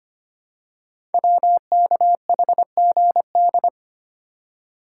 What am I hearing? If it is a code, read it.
Morse code, heard as a keyed tone, about 25 wpm, 700 Hz. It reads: WKHGD
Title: Code Groups - 5 Letters - 25wpm